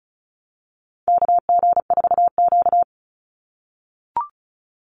Morse code, heard as a keyed tone, about 34 wpm, 700 Hz. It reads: KG4Q
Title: Call Signs - 35wpm